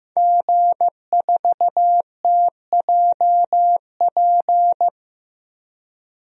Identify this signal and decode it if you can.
Morse code, heard as a keyed tone, about 15 wpm, 700 Hz. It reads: G4TJP
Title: DXCC-100 Call Signs - 5 Characters - 15wpm